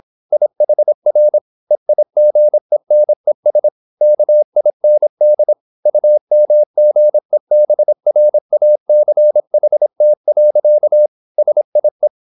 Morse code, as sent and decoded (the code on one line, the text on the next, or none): .. .... .-. / . .. --. -. ... / -.- .. -. -.. / ..- -- --. -... .-. .- -.-. .... - .-.-.- / ... .. .
IHR EIGENES KIND UMGEBRACHT. SIE